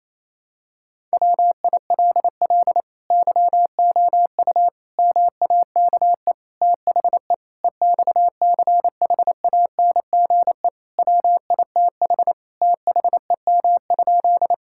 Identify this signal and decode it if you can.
Morse code, heard as a keyed tone, about 28 wpm, 700 Hz. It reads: WILL YOU MAKE THE EXCHANGE WITH THEM?
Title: Sentences from Top 1000 Words - Review Part 6 - 28wpm